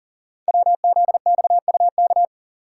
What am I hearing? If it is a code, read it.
Morse code, heard as a keyed tone, about 41 wpm, 700 Hz. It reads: WZXUK
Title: Code Groups - 5 Letter:Numbers - 40wpm